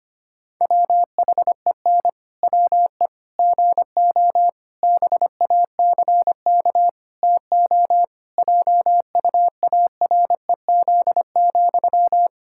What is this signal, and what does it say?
Morse code, heard as a keyed tone, about 25 wpm, 700 Hz. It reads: WHEN WE GO BACK TO JUAREZ,
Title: Pangram Challenge v2 - 25wpm